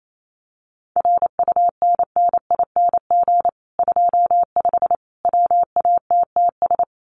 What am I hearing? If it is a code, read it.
Morse code, heard as a keyed tone, about 28 wpm, 700 Hz. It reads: RUNNING 25 WATTS
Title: Common QSO Phrases 28wpm